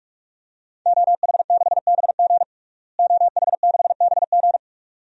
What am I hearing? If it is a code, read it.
Morse code, heard as a keyed tone, about 43 wpm, 700 Hz. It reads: OH6BG OH6BG
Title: DXCC-100 Call Signs - 5 Characters - Repeated 2x - 45wpm